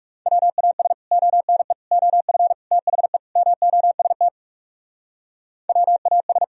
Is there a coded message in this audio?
Yes: Morse code from 0.3 to 6.4 s, heard as a keyed tone, about 40 wpm, 700 Hz. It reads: WAS ONE OF THE MOST WAS